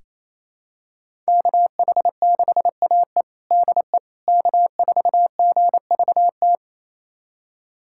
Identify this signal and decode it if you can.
Morse code, heard as a keyed tone, about 28 wpm, 700 Hz. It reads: KH6AE DE K4GVT
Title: USA Callsign DE Callsign - 5 Characters - 28wpm